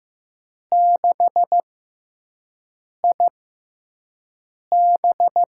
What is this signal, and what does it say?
Morse code, heard as a keyed tone, about 15 wpm, spaced as for 5 wpm, 700 Hz. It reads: 6IB
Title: USA Call Signs - 5 Characters - 6x Character Spacing - 15wpm